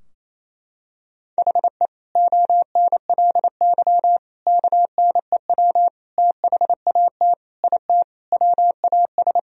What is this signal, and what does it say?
Morse code, heard as a keyed tone, about 27 wpm, 700 Hz. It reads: HE ONLY KNEW THAT IT WAS
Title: Sentences from Top 1000 Words - Review Part 2 - 28wpm